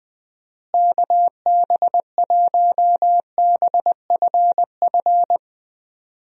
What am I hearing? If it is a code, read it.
Morse code, heard as a keyed tone, about 20 wpm, 700 Hz. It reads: KB1BFF
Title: USA Call Signs - 6 Characters - Repeated 3x - 20wpm